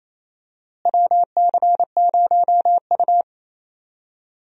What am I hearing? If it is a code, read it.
Morse code, heard as a keyed tone, about 27 wpm, 700 Hz. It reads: WC0U
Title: USA Call Signs - 4 Characters - Repeated 2x - 28wpm